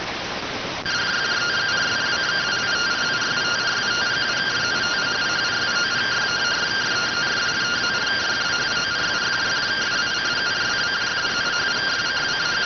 Two-tone teleprinter signal, about 170 Hz shift.